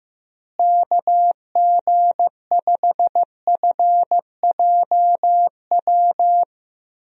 Morse code, heard as a keyed tone, about 15 wpm, 700 Hz. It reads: KG5FJW
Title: USA Call Signs - 6 Characters - Repeated 2x - 15wpm